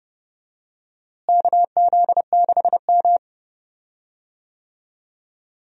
Morse code, heard as a keyed tone, about 32 wpm, 700 Hz. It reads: KZ6M
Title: USA Call Signs - 4 Characters - Repeated 3x - 30wpm